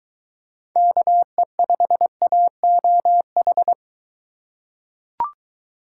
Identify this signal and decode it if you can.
Morse code, heard as a keyed tone, about 22 wpm, 700 Hz. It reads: KE5AOH